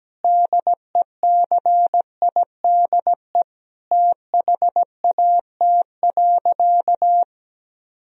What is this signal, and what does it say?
Morse code, heard as a keyed tone, about 17 wpm, 700 Hz. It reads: DECIDE THAT.
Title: Sentences from Top 1000 Words - Review Part 3 - 17wpm